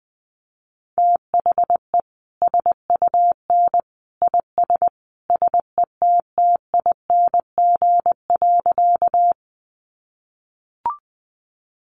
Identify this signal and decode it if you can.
Morse code, heard as a keyed tone, about 20 wpm, 700 Hz. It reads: THE SUN IS SETTING.
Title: Sentences from Top 1300 Words 20wpm